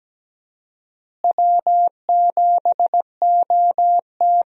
Morse code, heard as a keyed tone, about 17 wpm, 700 Hz. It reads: W7OT
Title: USA Call Signs - 4 Characters - Repeated 2x - 17wpm